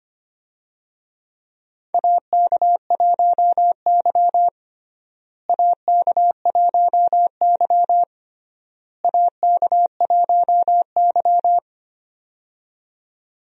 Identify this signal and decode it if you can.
Morse code, heard as a keyed tone, about 25 wpm, 700 Hz. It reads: AK1Y AK1Y AK1Y
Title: USA Call Signs - 4 Characters - Repeated 3x - 25wpm